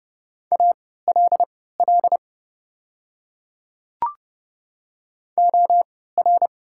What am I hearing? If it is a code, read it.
Morse code, heard as a keyed tone, about 32 wpm, spaced as for 17 wpm, 700 Hz. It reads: ALL OR
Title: 3x Character Spacing - Top 100 Words - 30wpm